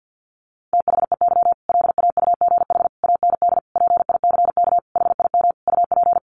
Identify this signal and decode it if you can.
Morse code, heard as a keyed tone, about 48 wpm, 700 Hz. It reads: THEY LAUGH AND PICK HIM UP
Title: Sentences from Top 1000 Words 50wpm